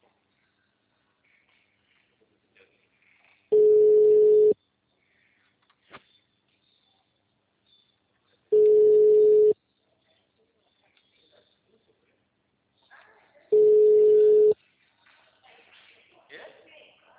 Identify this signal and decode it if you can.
Ringback tone (Europe, 425 Hz), with speech.